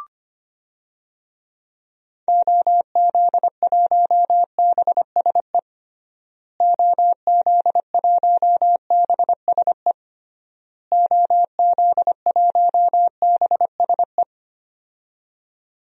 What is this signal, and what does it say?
Morse code, heard as a keyed tone, about 25 wpm, 700 Hz. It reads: OZ1BSE OZ1BSE OZ1BSE
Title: DXCC-100 Call Signs - 6 Characters - Repeated 3x - 25wpm